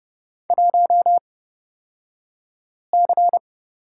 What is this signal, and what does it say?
Morse code, heard as a keyed tone, about 30 wpm, spaced as for 5 wpm, 700 Hz. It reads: 1C
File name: Lesson 37d - Letter x - Two Characters 30wpm